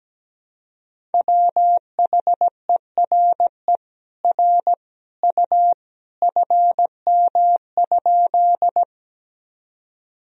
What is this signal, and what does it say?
Morse code, heard as a keyed tone, about 17 wpm, 700 Hz. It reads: WHERE R U FM?
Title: Sentences from Top 200 Words and Abbreviations 17wpm